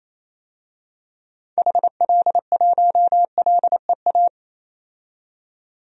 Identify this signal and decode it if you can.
Morse code, heard as a keyed tone, about 28 wpm, 700 Hz. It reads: HL1LEA
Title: DXCC-100 Call Signs - 6 Characters - 28wpm